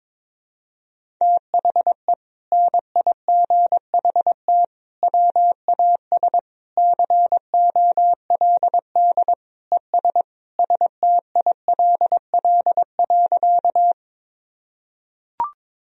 Morse code, heard as a keyed tone, about 22 wpm, 700 Hz. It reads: THE NIGHT WAS COLD ES STILL.
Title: Sentences from Top 400 Words and Abbreviations 22wpm